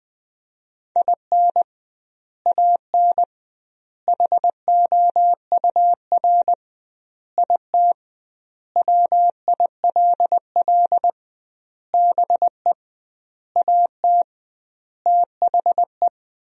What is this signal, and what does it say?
Morse code, heard as a keyed tone, about 20 wpm, 700 Hz. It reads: IN AN HOUR IT WILL BE AT THE
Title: 2x Word-Spacing - Sentences from Top 400 Words - 20wpm